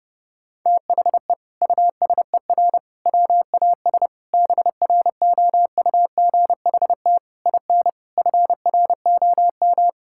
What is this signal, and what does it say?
Morse code, heard as a keyed tone, about 32 wpm, 700 Hz. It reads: THE USER WAS BROUGHT IN FROM
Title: Sentences from Top 1000 Words - Review Part 5 - 30wpm